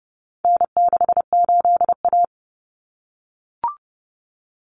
Morse code, heard as a keyed tone, about 30 wpm, 700 Hz. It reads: N68A
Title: Common Cut Numbers - 4-Digit 30wpm